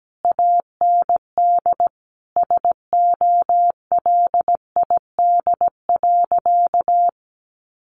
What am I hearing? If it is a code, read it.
Morse code, heard as a keyed tone, about 17 wpm, 700 Hz. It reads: AND SOLID.
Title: Sentences from Top 1500 Words 17wpm